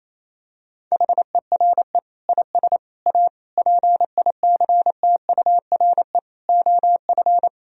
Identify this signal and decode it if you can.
Morse code, heard as a keyed tone, about 28 wpm, 700 Hz. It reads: HERE IS A PICTURE OF
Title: Sentences from Top 1000 Words - Review Part 1 - 28wpm